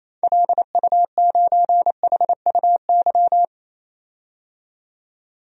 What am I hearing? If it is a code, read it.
Morse code, heard as a keyed tone, about 28 wpm, 700 Hz. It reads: LU9HUY
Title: DXCC-100 Call Signs - 6 Characters - 28wpm